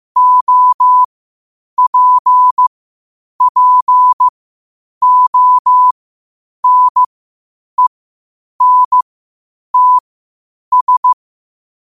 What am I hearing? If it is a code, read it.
Morse code, heard as a keyed tone, about 15 wpm, spaced as for 8 wpm, 1000 Hz. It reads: OPPONENTS
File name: qotdfiles / 8wpm